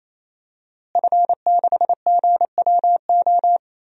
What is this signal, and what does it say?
Morse code, heard as a keyed tone, about 28 wpm, 700 Hz. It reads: F6GWO
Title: DXCC-100 Call Signs - 5 Characters - Repeated 3x - 28wpm